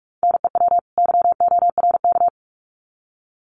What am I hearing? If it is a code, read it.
Morse code, heard as a keyed tone, about 44 wpm, 700 Hz. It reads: NEW YORK
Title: Sets of 2 Words 45wpm